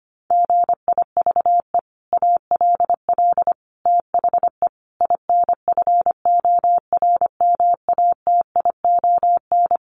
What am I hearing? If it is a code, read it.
Morse code, heard as a keyed tone, about 25 wpm, 700 Hz. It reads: GIVE ALL THE INFORMATION
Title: Sentences from Top 600 Words 25wpm